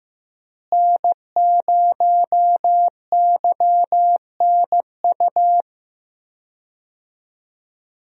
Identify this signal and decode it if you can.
Morse code, heard as a keyed tone, about 15 wpm, 700 Hz. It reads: N0YNU
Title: USA Call Signs - 5 Characters - Repeated 2x - 15wpm